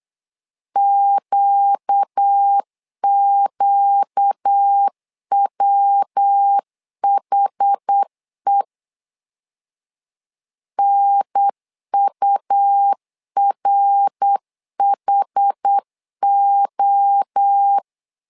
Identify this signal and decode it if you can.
Morse code, heard as a keyed tone, about 8 wpm, 800 Hz. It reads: QQWHE NURHO
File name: LEZIONI / 4 Q 8wpm 2732